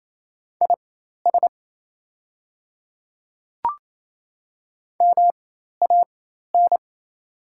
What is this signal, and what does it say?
Morse code, heard as a keyed tone, about 28 wpm, spaced as for 13 wpm, 690 Hz. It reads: IS MAN
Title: 4x Character Spacing - Top 200 Words - 28wpm